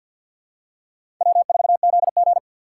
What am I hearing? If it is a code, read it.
Morse code, heard as a keyed tone, about 48 wpm, 700 Hz. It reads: W4ZG
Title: USA Call Signs - 4 Characters - 3x Character Spacing - 50wpm